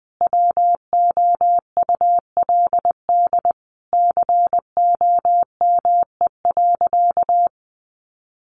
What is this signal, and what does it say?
Morse code, heard as a keyed tone, about 20 wpm, 700 Hz. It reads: WOULD COME.
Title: Sentences from Top 1500 Words 20wpm